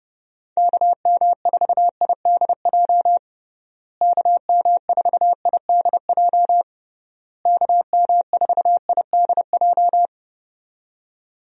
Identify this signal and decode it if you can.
Morse code, heard as a keyed tone, about 30 wpm, 700 Hz. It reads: KM4IDJ KM4IDJ KM4IDJ
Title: USA Call Signs - 6 Characters - Repeated 3x - 30wpm